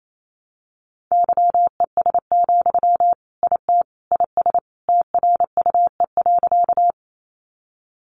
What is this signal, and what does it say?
Morse code, heard as a keyed tone, about 28 wpm, 700 Hz. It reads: YES, IT IS TRUE.
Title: Sentences from Top 500 Words 28wpm